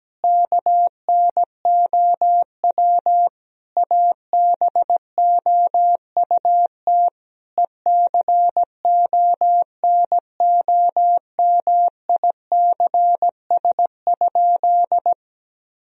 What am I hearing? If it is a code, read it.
Morse code, heard as a keyed tone, about 17 wpm, 700 Hz. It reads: KNOW ABOUT ECONOMICS?
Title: Sentences from Top 1000 Words - Review Part 3 - 17wpm